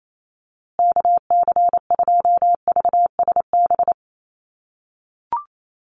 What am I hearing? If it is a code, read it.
Morse code, heard as a keyed tone, about 28 wpm, 700 Hz. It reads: KC2VSB
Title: USA Call Signs Repeated 3x 28wpm